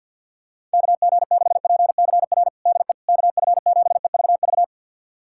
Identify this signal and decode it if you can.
Morse code, heard as a keyed tone, about 51 wpm, 700 Hz. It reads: KG6PCR DE KF7EVV
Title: USA Callsign DE Callsign - 6 Characters - 50wpm